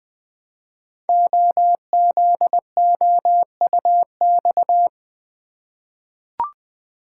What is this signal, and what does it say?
Morse code, heard as a keyed tone, about 20 wpm, 700 Hz. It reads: OZOUX
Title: Code Groups - 5 Letter:Numbers - 20wpm